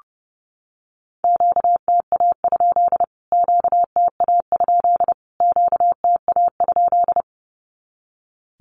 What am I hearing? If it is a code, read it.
Morse code, heard as a keyed tone, about 30 wpm, 700 Hz. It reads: QTA? QTA? QTA?
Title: Q-Codes Repeated 3x 30wpm